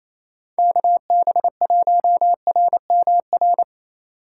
Morse code, heard as a keyed tone, about 35 wpm, 700 Hz. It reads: KB1RMR